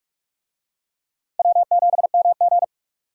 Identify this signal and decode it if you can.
Morse code, heard as a keyed tone, about 44 wpm, 700 Hz. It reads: W7MG